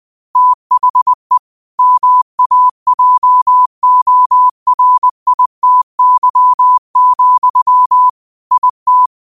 Morse code, heard as a keyed tone, about 20 wpm, 1000 Hz. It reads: THE MAJORITY, IT
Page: Daily quotes in Morse Code at 20 Words per Minute. Use this to improve your code copying skill.